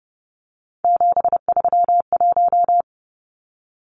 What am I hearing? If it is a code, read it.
Morse code, heard as a keyed tone, about 30 wpm, 700 Hz. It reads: 731
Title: Numbers - 3-Digit 30wpm